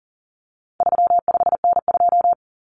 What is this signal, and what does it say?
Morse code, heard as a keyed tone, about 40 wpm, 700 Hz. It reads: 35N2
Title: Common Cut Numbers - 4-Digit 40wpm